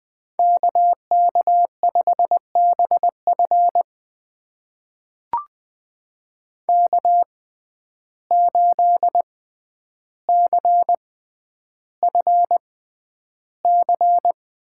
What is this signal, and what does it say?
Morse code, heard as a keyed tone, about 20 wpm, spaced as for 10 wpm, 700 Hz. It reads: KK5BF K8CFC
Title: USA Call Signs - 6x Character Spacing - 20wpm